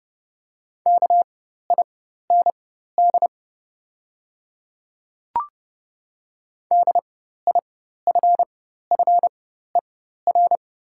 Morse code, heard as a keyed tone, about 30 wpm, spaced as for 14 wpm, 700 Hz. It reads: KIND DIFFER
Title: 4x Character Spacing - Top 300 Words - 30wpm